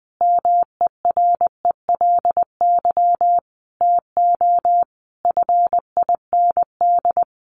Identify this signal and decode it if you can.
Morse code, heard as a keyed tone, about 20 wpm, 700 Hz. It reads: MERELY TO FIND
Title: Sentences from Top 1400 Words 20wpm